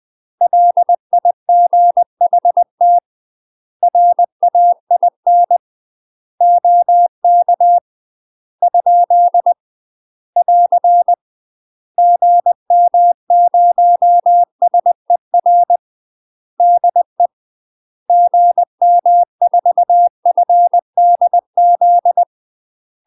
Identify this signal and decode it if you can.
Morse code, heard as a keyed tone, about 20 wpm, 700 Hz. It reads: LIGHT RAIN OK ? <AR> GM0SER DE GM4FDZ
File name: UK MASTER QSOs TRAINING/20WPM